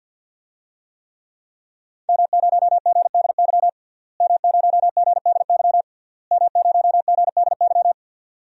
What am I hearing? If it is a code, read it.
Morse code, heard as a keyed tone, about 52 wpm, 700 Hz. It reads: M0GDY M0GDY M0GDY
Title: DXCC-100 Call Signs - 5 Characters - Repeated 3x - 50wpm